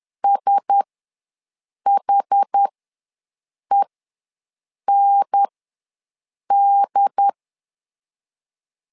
Morse code, heard as a keyed tone, about 10 wpm, spaced as for 6 wpm, 800 Hz. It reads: SHEND
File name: LEZIONI / 4 Q1S ws 10wpm 2208